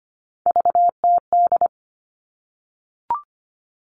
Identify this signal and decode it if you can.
Morse code, heard as a keyed tone, about 25 wpm, 700 Hz. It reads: VTD
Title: All Cut Numbers - 3-Digit 25wpm